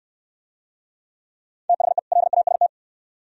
Morse code, heard as a keyed tone, about 68 wpm, 700 Hz. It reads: THE LAST